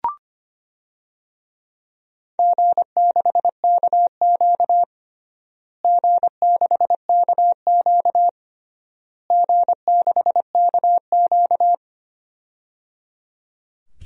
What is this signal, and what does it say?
Morse code, heard as a keyed tone, about 25 wpm, 700 Hz. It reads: G6KQ G6KQ G6KQ
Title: DXCC-100 Call Signs - 4 Characters - Repeated 3x - 25wpm